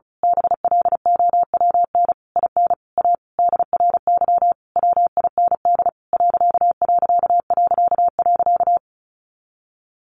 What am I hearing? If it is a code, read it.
Morse code, heard as a keyed tone, about 34 wpm, 700 Hz. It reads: BLOWN IN A DRY WIND ....
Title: A Sound Of Thunder 35wpm